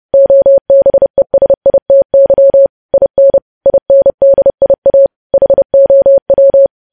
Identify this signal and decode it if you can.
Morse code, heard as a keyed tone, about 30 wpm, 550 Hz. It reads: OBESITY IN INDIA HOW
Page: news headlines in morse code at 30 wpm